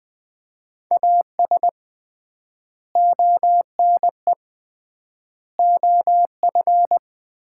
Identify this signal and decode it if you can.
Morse code, heard as a keyed tone, about 20 wpm, 690 Hz. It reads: AS ONE OF
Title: 3x Word-Spacing - Sets of 5 Words - 20wpm